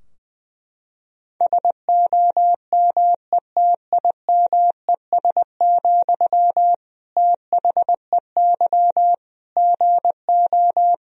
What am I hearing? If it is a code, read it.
Morse code, heard as a keyed tone, about 20 wpm, 700 Hz. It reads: SOMETIMES, THEY GO
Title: Sentences from Top 1000 Words - Review Part 3 - 20wpm